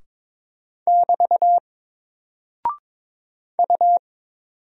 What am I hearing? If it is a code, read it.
Morse code, heard as a keyed tone, about 22 wpm, 700 Hz. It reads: <BT> U
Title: Single Character (Letter:Number:Punctuation:Prosign) Alt 22wpm